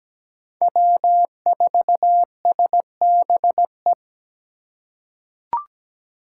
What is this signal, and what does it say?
Morse code, heard as a keyed tone, about 17 wpm, 700 Hz. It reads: W4SBE
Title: USA Call Signs - 5 Characters - 4x Character Spacing - 17wpm